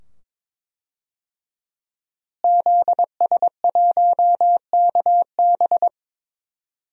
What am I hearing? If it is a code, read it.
Morse code, heard as a keyed tone, about 22 wpm, 700 Hz. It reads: ZS1KB
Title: DXCC-100 Call Signs - 5 Characters - Repeated 2x - 22wpm